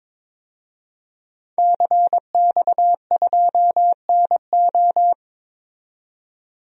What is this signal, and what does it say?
Morse code, heard as a keyed tone, about 21 wpm, 700 Hz. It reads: CX2NO
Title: DXCC-100 Call Signs - 5 Characters - 22wpm